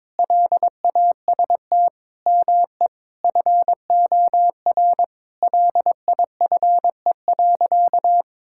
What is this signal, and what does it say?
Morse code, heard as a keyed tone, about 22 wpm, 700 Hz. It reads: LAST ME FOR LIFE.
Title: Sentences from Top 1000 Words - Review Part 6 - 22wpm